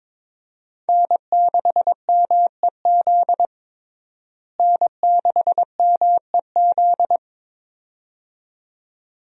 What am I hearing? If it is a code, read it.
Morse code, heard as a keyed tone, about 21 wpm, 700 Hz. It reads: N6MEZ N6MEZ
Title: USA Call Signs - 5 Characters - Repeated 2x - 22wpm